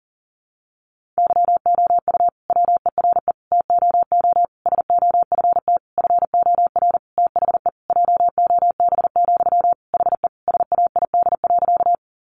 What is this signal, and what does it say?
Morse code, heard as a keyed tone, about 40 wpm, 700 Hz. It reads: YOU WERE TOO SOFT FOR THE JOB, HE SAID.
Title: Sentences from Top 1500 Words 40wpm